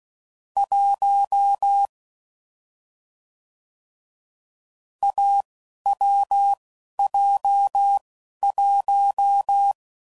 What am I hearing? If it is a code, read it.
Morse code, heard as a keyed tone, about 16 wpm, spaced as for 11 wpm, 800 Hz. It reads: 1 AWJ1
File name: Combinations / Difficult Combos / AWJ1